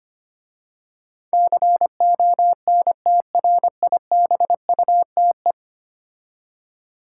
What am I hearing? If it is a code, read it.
Morse code, heard as a keyed tone, about 25 wpm, 690 Hz. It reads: CONTRIBUTE
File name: 10 Letter Words - First 3 Letters Given - 25wpm